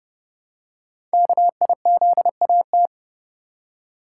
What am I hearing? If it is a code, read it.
Morse code, heard as a keyed tone, about 30 wpm, 700 Hz. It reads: KIZAT
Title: Code Groups - 5 Letters - 30wpm